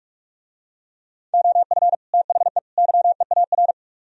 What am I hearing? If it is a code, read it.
Morse code, heard as a keyed tone, about 46 wpm, 700 Hz. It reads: OF THE YEAR